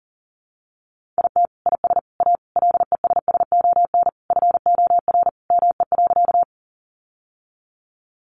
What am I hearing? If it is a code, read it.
Morse code, heard as a keyed tone, about 40 wpm, 700 Hz. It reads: IT IS A LESSON FOR ME.
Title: Sentences from Top 1200 Words 40wpm